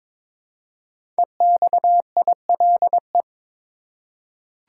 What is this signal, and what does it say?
Morse code, heard as a keyed tone, about 22 wpm, 700 Hz. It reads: EXILE
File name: 5 Letter Words - First 3 Letters Given - 22wpm